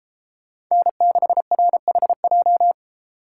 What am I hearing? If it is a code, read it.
Morse code, heard as a keyed tone, about 33 wpm, 700 Hz. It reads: N6RHJ